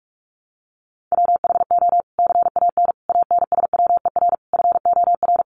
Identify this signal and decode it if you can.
Morse code, heard as a keyed tone, about 44 wpm, 700 Hz. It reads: WHO CAN ANSWER FOR
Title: Sentences from Top 300 Words 45wpm